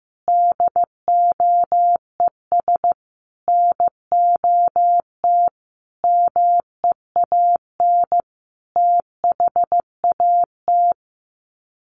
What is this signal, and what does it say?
Morse code, heard as a keyed tone, about 15 wpm, 700 Hz. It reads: DOES NOT MEAN THAT
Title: Sets of 5 Words Encore 15wpm